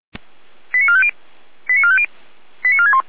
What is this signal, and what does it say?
Five-tone selective call (ZVEI), heard as quick runs of five tones, tones of about 70 ms. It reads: 89450 89450 89461
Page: /455/SDR/ZVEI/88335/